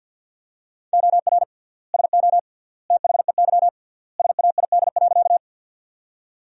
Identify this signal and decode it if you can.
Morse code, heard as a keyed tone, about 50 wpm, 700 Hz. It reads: OR SO THEY SAID.
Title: 3x Word-Spacing - Sentences from Top 100 Words - 50wpm